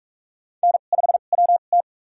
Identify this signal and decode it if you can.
Morse code, heard as a keyed tone, about 46 wpm, spaced as for 32 wpm, 700 Hz. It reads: N5WT
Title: USA Call Signs - 4 Characters - 2x Character Spacing - 45wpm